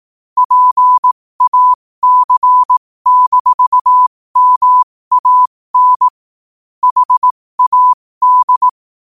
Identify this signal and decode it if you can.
Morse code, heard as a keyed tone, about 18 wpm, spaced as for 15 wpm, 1000 Hz. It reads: PAC-MAN HAD
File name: qotdfiles / 15wpm